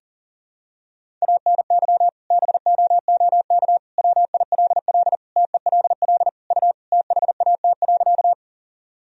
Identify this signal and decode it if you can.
Morse code, heard as a keyed tone, about 40 wpm, 700 Hz. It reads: ANY BOOK WILL TELL U THAT.
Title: Sentences from Top 300 Words and Abbreviations 40wpm